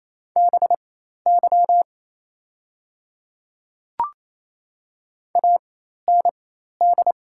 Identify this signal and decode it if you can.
Morse code, heard as a keyed tone, about 28 wpm, spaced as for 13 wpm, 700 Hz. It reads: BY AND
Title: 4x Character Spacing - Top 100 Words - 28wpm